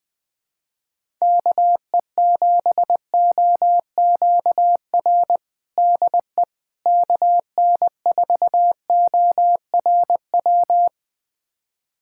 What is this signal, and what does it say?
Morse code, heard as a keyed tone, about 21 wpm, 700 Hz. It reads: KE7OQR DE KN4ORW
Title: USA Callsign DE Callsign - 6 Characters - 20wpm